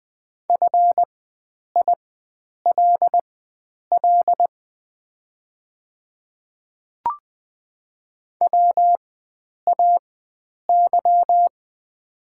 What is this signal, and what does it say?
Morse code, heard as a keyed tone, about 20 wpm, spaced as for 9 wpm, 700 Hz. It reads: FILL WAY
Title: 4x Character Spacing - Top 500 Words - 20wpm